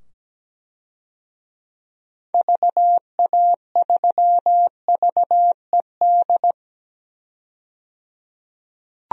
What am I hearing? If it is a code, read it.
Morse code, heard as a keyed tone, about 17 wpm, 700 Hz. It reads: VA3VED
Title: DXCC-100 Call Signs - 6 Characters - 17wpm